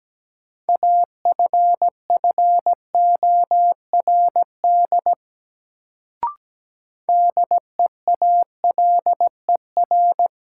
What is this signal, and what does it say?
Morse code, heard as a keyed tone, about 17 wpm, 700 Hz. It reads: AFFORD DEALER
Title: Sentences from Top 1100 Words - New Word Drill Alt - 17wpm